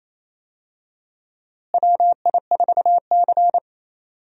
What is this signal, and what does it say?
Morse code, heard as a keyed tone, about 28 wpm, 700 Hz. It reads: WI4C
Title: USA Call Signs - 4 Characters - Repeated 3x - 28wpm